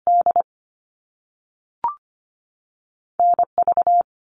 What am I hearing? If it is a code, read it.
Morse code, heard as a keyed tone, about 25 wpm, 700 Hz. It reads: D NV
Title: US States 25wpm